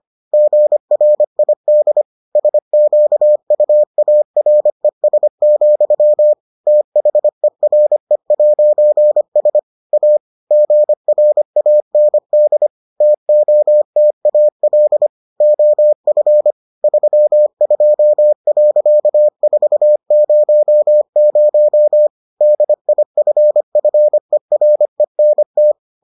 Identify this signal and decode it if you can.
Morse code, heard as a keyed tone, about 25 wpm, 600 Hz. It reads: GRID SQUARES, THERE'S A GRAND TOTAL OF 32.400 DIFFERENT